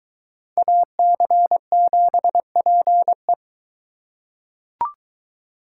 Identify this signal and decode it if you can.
Morse code, heard as a keyed tone, about 22 wpm, 700 Hz. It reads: AC7PE